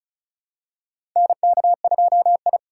Morse code, heard as a keyed tone, about 36 wpm, 700 Hz. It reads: NK2I
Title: USA Call Signs - 4 Characters - 3x Character Spacing - 35wpm